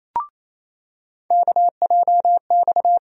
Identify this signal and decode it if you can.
Morse code, heard as a keyed tone, about 28 wpm, 700 Hz. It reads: KJX
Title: USA Call Sign Postfixes - 3 Letters 28wpm